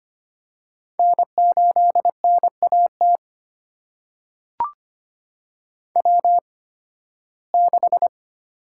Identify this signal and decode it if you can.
Morse code, heard as a keyed tone, about 25 wpm, spaced as for 16 wpm, 700 Hz. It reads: N8NAT W6
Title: USA Call Signs - 5 Characters - 8x Character Spacing - 25wpm